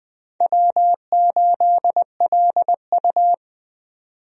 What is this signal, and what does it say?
Morse code, heard as a keyed tone, about 21 wpm, 700 Hz. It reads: W8LU
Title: USA Call Signs - 4 Characters - 20wpm